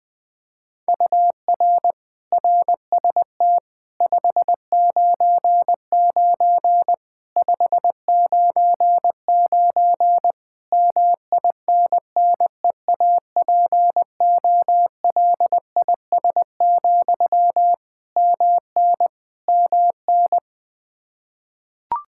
Morse code, heard as a keyed tone, about 21 wpm, 700 Hz. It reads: UR RST 599 599 MINNEAPOLIS, MN MN
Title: Practice QSO - Exchange 1 - Beginning - 20wpm